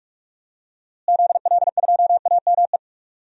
Morse code, heard as a keyed tone, about 46 wpm, 700 Hz. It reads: ZL2AME
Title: DXCC-100 Call Signs - 6 Characters - Repeated 2x - 45wpm